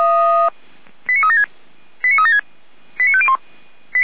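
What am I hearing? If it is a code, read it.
Five-tone selective call (ZVEI), heard as quick runs of five tones, tones of about 70 ms. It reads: 89376 89376 89501
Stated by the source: ZVEI five-tone selective call 89310 received by SDR.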